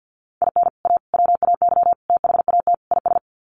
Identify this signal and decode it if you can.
Morse code, heard as a keyed tone, about 50 wpm, 700 Hz. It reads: IN A WAY THAT IS
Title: Sets of 5 Words Encore 50wpm